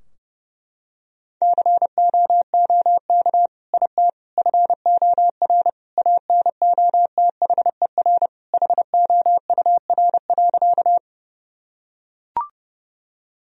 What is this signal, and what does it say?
Morse code, heard as a keyed tone, about 30 wpm, 700 Hz. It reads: COOK IT FOR ANOTHER HOUR.
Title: Sentences from Top 1000 Words - Review Part 4 - 30wpm